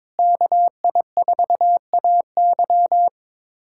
Morse code, heard as a keyed tone, about 22 wpm, 700 Hz. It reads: KI4AY
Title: USA Callsign DE Callsign - 5 Characters - 22wpm